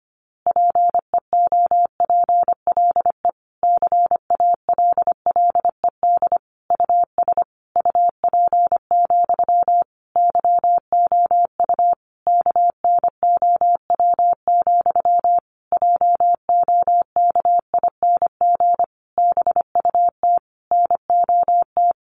Morse code, heard as a keyed tone, about 25 wpm, 700 Hz. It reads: PEOPLE CALLED US UP, YOU KNOW, JOKING BUT NOT
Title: A Sound Of Thunder 25wpm